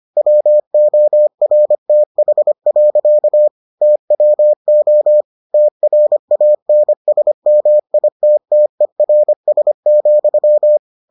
Morse code, heard as a keyed tone, about 25 wpm, 600 Hz. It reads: WORTH. TWO TRANSMITTERS,